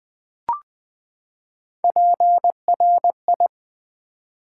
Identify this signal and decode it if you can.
Morse code, heard as a keyed tone, about 20 wpm, 700 Hz. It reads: PRI
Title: Build A Word - 3 Letters At A Time - 9-Letter Words - 20wpm